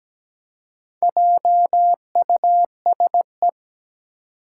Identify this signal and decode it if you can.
Morse code, heard as a keyed tone, about 17 wpm, 700 Hz. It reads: JUSE
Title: Four Challenging Letter-Number Combinations 17wpm